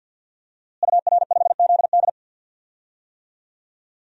Morse code, heard as a keyed tone, about 51 wpm, 700 Hz. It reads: UR5ZD
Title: DXCC-100 Call Signs - 5 Characters - Repeated 2x - 50wpm